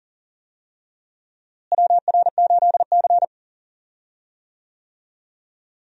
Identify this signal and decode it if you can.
Morse code, heard as a keyed tone, about 40 wpm, 700 Hz. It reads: WR8C
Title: USA Call Signs - 4 Characters - Repeated 2x - 40wpm